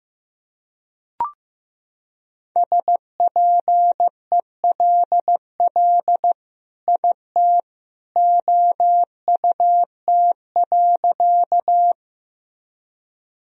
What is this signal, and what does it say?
Morse code, heard as a keyed tone, about 15 wpm, 700 Hz. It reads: SPELL IT OUT.
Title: Sentences from Top 1000 Words - Review Part 1 - 15wpm